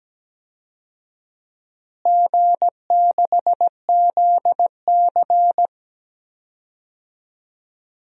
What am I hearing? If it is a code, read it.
Morse code, heard as a keyed tone, about 17 wpm, 700 Hz. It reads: G6ZC
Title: DXCC-100 Call Signs - 4 Characters - Repeated 2x - 17wpm